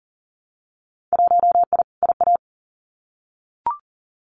Morse code, heard as a keyed tone, about 40 wpm, 700 Hz. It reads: 1I IA
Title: Winter Field Day Exchanges 40wpm